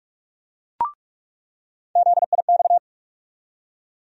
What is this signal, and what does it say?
Morse code, heard as a keyed tone, about 45 wpm, 700 Hz. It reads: ZIX
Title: USA Call Sign Postfixes - 3 Letters 45wpm